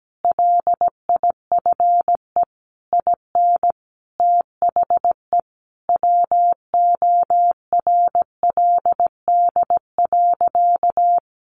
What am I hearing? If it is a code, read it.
Morse code, heard as a keyed tone, about 17 wpm, 700 Hz. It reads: LIFE IN THE WORLD.
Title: Sentences from Top 1100 Words 17wpm